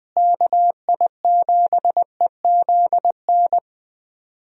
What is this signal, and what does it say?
Morse code, heard as a keyed tone, about 20 wpm, 700 Hz. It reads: KI7EZN
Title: USA Call Signs - 6 Characters - Repeated 2x - 20wpm